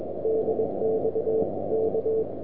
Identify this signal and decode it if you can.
Morse code, heard as a keyed tone, about 21 wpm, 440 Hz, with noise. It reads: DXK
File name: Recordings / nl7z 160 1740z